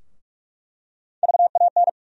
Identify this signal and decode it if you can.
Morse code, heard as a keyed tone, about 46 wpm, 700 Hz. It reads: VAN
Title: USA Call Sign Postfixes - 3 Letters 45wpm